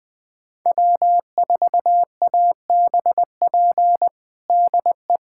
Morse code, heard as a keyed tone, about 20 wpm, 700 Hz. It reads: W4ABP DE
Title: USA Callsign DE Callsign 20wpm